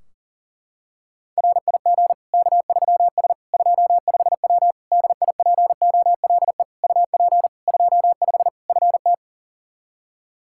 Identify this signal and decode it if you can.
Morse code, heard as a keyed tone, about 37 wpm, 700 Hz. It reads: RIG K3S 25W DIPOLE UP 25 FT
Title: K1USN SST QSO Practice 40wpm